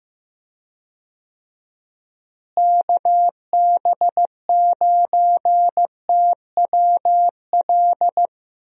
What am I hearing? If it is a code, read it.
Morse code, heard as a keyed tone, about 15 wpm, 700 Hz. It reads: KB9TWL
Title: USA Call Signs - 6 Characters - Repeated 2x - 15wpm